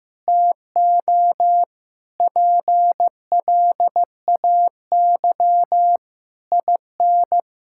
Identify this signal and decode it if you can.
Morse code, heard as a keyed tone, about 16 wpm, 700 Hz. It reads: TO PLAY IN
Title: Sentences from Top 1000 Words - Review Part 6 - 15wpm